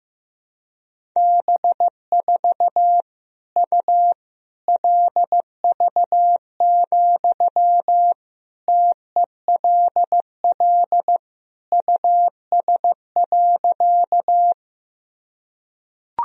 Morse code, heard as a keyed tone, about 15 wpm, 700 Hz. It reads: B4 U LV, TELL US.
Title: Sentences from Top 400 Words and Abbreviations 15wpm